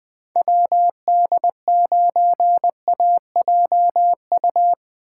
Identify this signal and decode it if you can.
Morse code, heard as a keyed tone, about 20 wpm, 700 Hz. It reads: WD9AJU
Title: Speed Racing - USA 2x3 Call Signs 17wpm